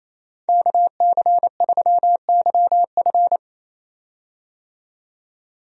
Morse code, heard as a keyed tone, about 27 wpm, 700 Hz. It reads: KC3YF
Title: USA Call Signs - 5 Characters - 28wpm